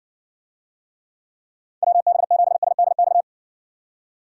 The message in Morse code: WB7SDX

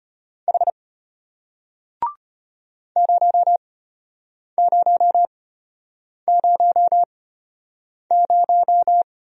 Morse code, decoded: H 0000

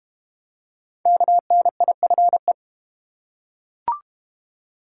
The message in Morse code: KNIFE

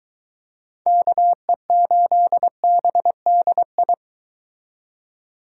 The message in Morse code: KE8BDI

